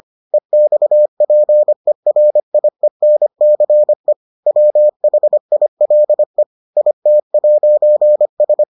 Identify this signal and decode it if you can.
Morse code, heard as a keyed tone, about 25 wpm, 600 Hz. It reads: EXPERIENCE WHILE IT'S